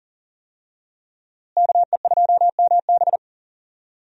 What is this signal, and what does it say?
Morse code, heard as a keyed tone, about 37 wpm, 700 Hz. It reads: KE2MB
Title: USA Call Signs - 5 Characters - Repeated 2x - 40wpm